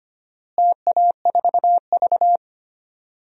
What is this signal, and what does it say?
Morse code, heard as a keyed tone, about 25 wpm, 700 Hz. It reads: TA4V
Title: DXCC-100 Call Signs - 4 Characters - Repeated 3x - 25wpm